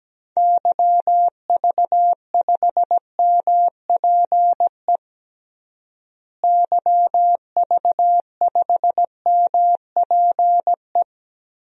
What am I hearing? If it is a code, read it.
Morse code, heard as a keyed tone, about 17 wpm, 700 Hz. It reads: YV5MPE YV5MPE
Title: DXCC-100 Call Signs - 6 Characters - Repeated 3x - 17wpm